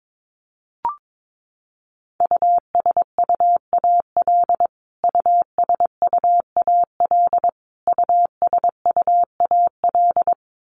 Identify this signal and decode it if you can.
Morse code, heard as a keyed tone, about 22 wpm, 700 Hz. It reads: USUAL USUAL USUAL
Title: Top 500 Words Repeated 22wpm